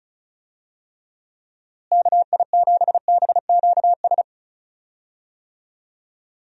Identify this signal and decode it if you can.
Morse code, heard as a keyed tone, about 34 wpm, 700 Hz. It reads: KI7BQS
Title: USA Call Signs - 6 Characters - Repeated 3x - 35wpm